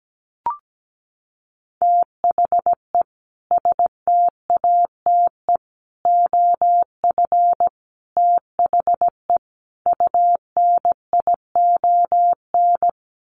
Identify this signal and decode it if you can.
Morse code, heard as a keyed tone, about 17 wpm, 700 Hz. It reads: THE STATE OF THE UNION
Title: Sets of 5 Words Encore 17wpm